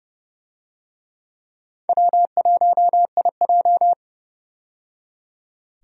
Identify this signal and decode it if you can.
Morse code, heard as a keyed tone, about 31 wpm, 700 Hz. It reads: W1IJ
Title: USA Call Signs - 4 Characters - 30wpm